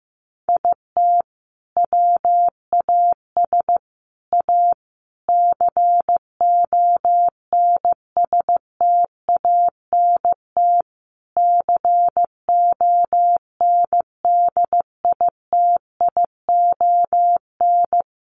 Morse code, heard as a keyed tone, about 15 wpm, 700 Hz. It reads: IT WAS A CONSTANT CONDITION
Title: Sentences from Top 1500 Words 15wpm